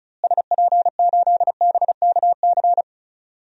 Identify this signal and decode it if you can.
Morse code, heard as a keyed tone, about 34 wpm, 700 Hz. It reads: SP8BKC
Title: DXCC-100 Call Signs - 6 Characters - Repeated 2x - 35wpm